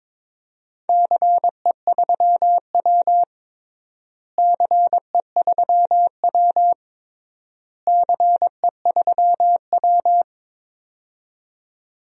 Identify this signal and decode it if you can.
Morse code, heard as a keyed tone, about 22 wpm, 700 Hz. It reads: CE3W CE3W CE3W
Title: DXCC-100 Call Signs - 4 Characters - Repeated 3x - 22wpm